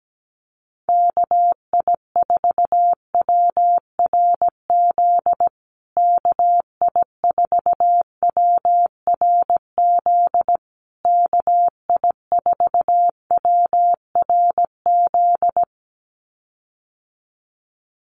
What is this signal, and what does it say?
Morse code, heard as a keyed tone, about 17 wpm, 700 Hz. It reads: KI4WRZ KI4WRZ KI4WRZ
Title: USA Call Signs Repeated 3x 17wpm